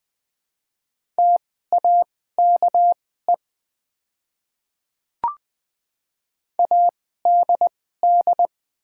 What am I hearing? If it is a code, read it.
Morse code, heard as a keyed tone, about 20 wpm, spaced as for 14 wpm, 700 Hz. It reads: TAKE ADD
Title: 2x Character Spacing - Top 200 Words - 20wpm